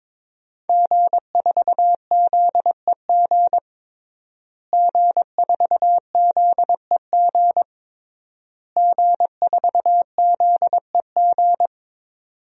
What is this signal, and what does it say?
Morse code, heard as a keyed tone, about 22 wpm, 700 Hz. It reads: G4ZEG G4ZEG G4ZEG
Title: DXCC-100 Call Signs - 5 Characters - Repeated 3x - 22wpm